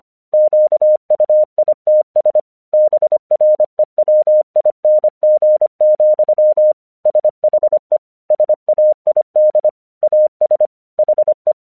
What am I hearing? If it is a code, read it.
Morse code, heard as a keyed tone, about 25 wpm, 610 Hz. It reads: QUITS BREWING, SHE SAID AS HE
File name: 1 English 25wpm / AQC22